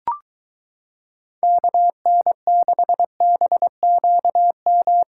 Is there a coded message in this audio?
Yes: Morse code from 1.4 to 5.0 s, heard as a keyed tone, about 23 wpm, 700 Hz. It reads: KN6BQM